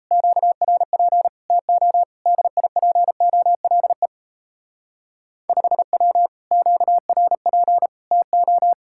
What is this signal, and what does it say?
Morse code, heard as a keyed tone, about 35 wpm, 700 Hz. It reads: QRP TO DIPOLE 5W QRP TO
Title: Speed-Racing Common QSO Phrases - 28wpm